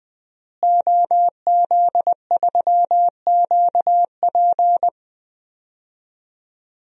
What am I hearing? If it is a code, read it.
Morse code, heard as a keyed tone, about 20 wpm, 700 Hz. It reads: OZ3QP
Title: DXCC-100 Call Signs - 5 Characters - Repeated 3x - 20wpm